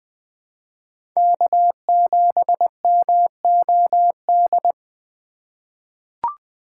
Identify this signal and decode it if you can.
Morse code, heard as a keyed tone, about 20 wpm, 700 Hz. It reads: K7MOD
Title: USA Call Signs - 5 Characters - 2x Character Spacing - 20wpm